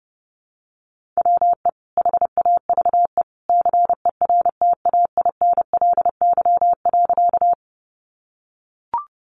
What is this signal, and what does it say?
Morse code, heard as a keyed tone, about 30 wpm, 700 Hz. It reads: WE HAVE CERTAINLY.
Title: Sentences from Top 900 Words 30wpm